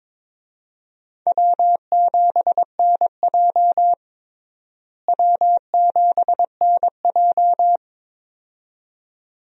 Morse code, heard as a keyed tone, about 22 wpm, 700 Hz. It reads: W7NJ W7NJ
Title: USA Call Signs - 4 Characters - Repeated 2x - 22wpm